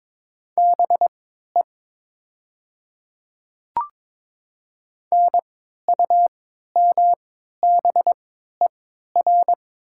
Morse code, heard as a keyed tone, about 22 wpm, spaced as for 12 wpm, 700 Hz. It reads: BE NUMBER
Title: 3x Character Spacing - Top 100 Words - 22wpm